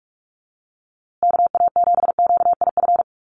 Morse code, heard as a keyed tone, about 44 wpm, 700 Hz. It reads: KA7QIF
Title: USA Call Signs Repeated 2x 45wpm